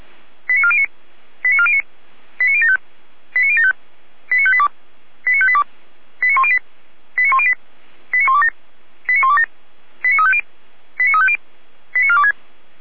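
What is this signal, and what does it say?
Five-tone selective call (ZVEI), heard as quick runs of five tones, tones of about 70 ms. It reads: 89409 89409 89075 89075 89582 89582 89108 89108 89127 89127 89350 89350 89426
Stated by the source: Five-tone ZVEI selective call received on an SDR.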